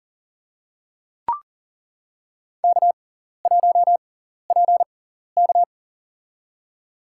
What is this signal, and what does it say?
Morse code, heard as a keyed tone, about 40 wpm, spaced as for 14 wpm, 700 Hz. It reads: K1PK
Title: USA Call Signs - 6x Character Spacing - 40wpm